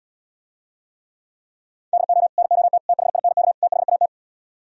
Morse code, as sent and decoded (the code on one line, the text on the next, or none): -.. --- / -. --- - / - .... .. -. -.- / - .... .- -
DO NOT THINK THAT